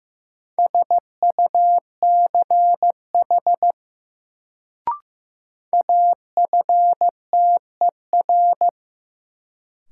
Morse code, heard as a keyed tone, about 15 wpm, 700 Hz. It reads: SUCH AFTER
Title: Sentences from Top 200 Words - New Word Drill - 15wpm